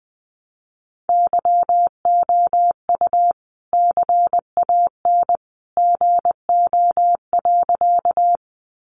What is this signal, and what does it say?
Morse code, heard as a keyed tone, about 20 wpm, 700 Hz. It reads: YOU CAN GO.
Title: Sentences from Top 100 Words 20wpm